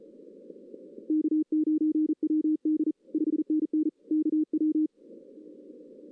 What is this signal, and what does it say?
Morse code, heard as a keyed tone, about 35 wpm, 310 Hz. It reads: K9WD 5NN KW
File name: 2016 arrl dx cw / 2016-ARRL-DX-CW-K5TR- - 0-3